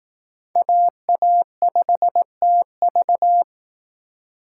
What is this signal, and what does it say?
Morse code, heard as a keyed tone, about 19 wpm, 700 Hz. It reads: AA5TV
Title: Speed Racing - USA 2x2 Call Signs 15wpm